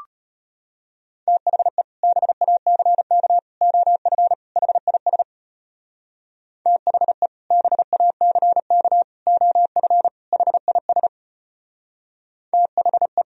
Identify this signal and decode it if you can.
Morse code, heard as a keyed tone, about 33 wpm, 700 Hz. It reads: THE BACK OF HIS THE BACK OF HIS THE